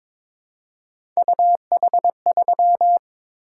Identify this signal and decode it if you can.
Morse code, heard as a keyed tone, about 22 wpm, 700 Hz. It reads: UH3
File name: practice / 22wpm / Three Letter-Number Combinations - 22wpm